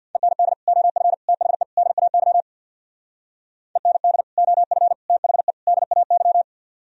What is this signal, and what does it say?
Morse code, heard as a keyed tone, about 64 wpm, 700 Hz. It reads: END OF THE DAY END OF THE DAY